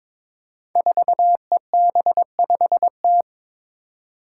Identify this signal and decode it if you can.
Morse code, heard as a keyed tone, about 22 wpm, 700 Hz. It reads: 4EB5T
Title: Five Challenging Letter-Number Combinations 22wpm